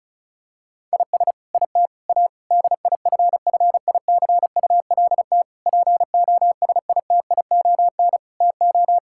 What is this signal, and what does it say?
Morse code, heard as a keyed tone, about 34 wpm, 700 Hz. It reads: IS IT A DIFFICULT POSITION TO
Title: Sentences from Top 1000 Words - Review Part 5 - 35wpm